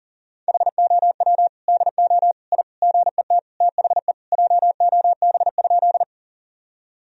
Morse code, heard as a keyed tone, about 40 wpm, 700 Hz. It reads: HOW DO I GET THE JOB?
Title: Sentences from Top 1000 Words - Review Part 3 - 40wpm